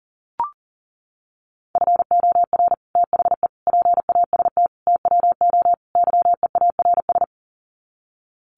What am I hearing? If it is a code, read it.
Morse code, heard as a keyed tone, about 40 wpm, 700 Hz. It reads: FOR THE PAST TWO YEARS
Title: Sets of 5 Words Encore2 40wpm